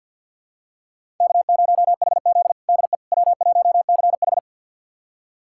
Morse code, heard as a keyed tone, about 46 wpm, 700 Hz. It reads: K0HZ DE W1CH
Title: USA Callsign DE Callsign - 4 Characters - 50wpm